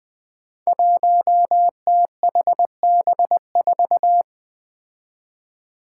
Morse code, heard as a keyed tone, about 20 wpm, 700 Hz. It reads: 1THB4
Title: Five Challenging Letter-Number Combinations 20wpm